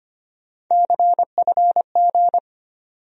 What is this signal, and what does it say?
Morse code, heard as a keyed tone, about 25 wpm, 700 Hz. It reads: CFG
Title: USA Call Sign Postfixes - 3 Letters 25wpm